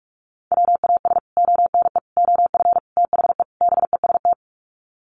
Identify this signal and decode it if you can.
Morse code, heard as a keyed tone, about 44 wpm, 700 Hz. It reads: WAS ONE OF THE BEST
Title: Sets of 5 Words Encore2 45wpm